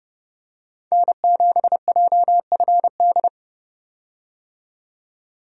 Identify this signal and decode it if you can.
Morse code, heard as a keyed tone, about 30 wpm, 700 Hz. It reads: N7JFD
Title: USA Call Signs - 5 Characters - 30wpm